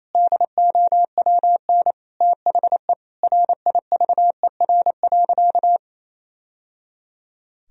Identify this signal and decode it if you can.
Morse code, heard as a keyed tone, about 27 wpm, 700 Hz. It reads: DOWN THE RIVER.
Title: Sentences from Top 1000 Words - Review Part 2 - 28wpm